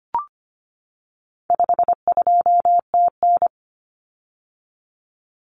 Morse code, heard as a keyed tone, about 25 wpm, 700 Hz. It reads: 52TN
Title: Common Cut Numbers - 4-Digit 25wpm